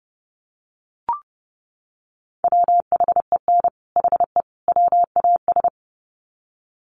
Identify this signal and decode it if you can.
Morse code, heard as a keyed tone, about 30 wpm, 700 Hz. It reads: WHEN HE WAS
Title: Sets of 3 Words 30wpm